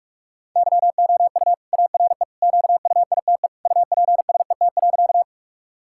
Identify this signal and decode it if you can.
Morse code, heard as a keyed tone, about 42 wpm, 700 Hz. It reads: YOU ARE QUITE UPSET.
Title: Sentences from Top 1000 Words - Review Part 5 - 45wpm